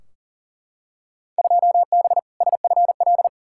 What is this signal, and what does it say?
Morse code, heard as a keyed tone, about 38 wpm, 700 Hz. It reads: 2B SFL
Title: Field Day Exchanges - 40wpm